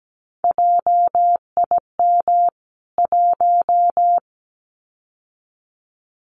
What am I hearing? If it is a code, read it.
Morse code, heard as a keyed tone, about 17 wpm, 700 Hz. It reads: JIM 1
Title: CWT Exchange 17wpm